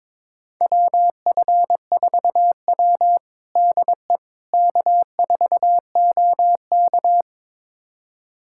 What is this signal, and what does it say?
Morse code, heard as a keyed tone, about 22 wpm, 700 Hz. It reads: WF4W DE K4OK
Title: USA Callsign DE Callsign - 4 Characters - 22wpm